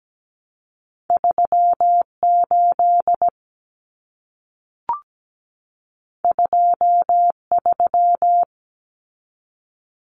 Morse code, heard as a keyed tone, about 17 wpm, 700 Hz. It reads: 38 23
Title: Common Cut Numbers - 2-Digit 17wpm